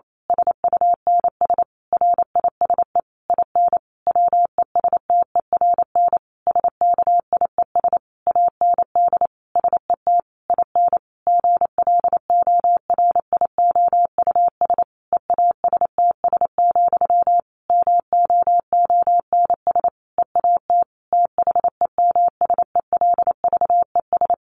Morse code, heard as a keyed tone, about 28 wpm, 700 Hz. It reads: SUNS RISE IN WESTERN SKIES AND SET IN GLORIOUS EASTS, MOONS EAT THEMSELVES
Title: A Sound Of Thunder 28wpm